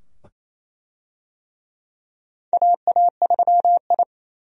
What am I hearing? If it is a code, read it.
Morse code, heard as a keyed tone, about 28 wpm, 700 Hz. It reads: AA3I
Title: USA Call Signs - 4 Characters - 28wpm